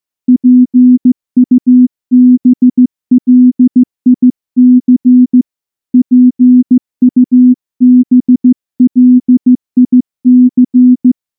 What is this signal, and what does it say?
Morse code, heard as a keyed tone, about 14 wpm, 250 Hz. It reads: PUBLIC PUBLIC
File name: morse-training-15WPM